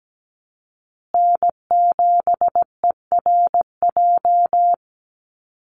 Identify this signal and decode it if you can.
Morse code, heard as a keyed tone, about 17 wpm, 700 Hz. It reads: N7ERJ
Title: USA Call Signs Repeated 2x 17wpm